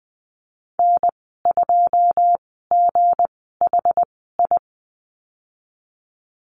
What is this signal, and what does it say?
Morse code, heard as a keyed tone, about 20 wpm, spaced as for 15 wpm, 700 Hz. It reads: N2GHI
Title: 213 Lesson 34g - Number 8 - Call Signs 20wpm